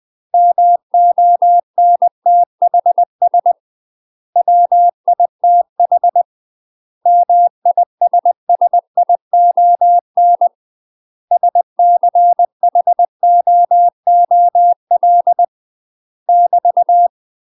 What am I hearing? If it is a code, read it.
Morse code, heard as a keyed tone, about 20 wpm, 700 Hz. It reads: MONTHS WITH MISSION SCHOOL <BT>
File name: UK MASTER QSOs TRAINING/20WPM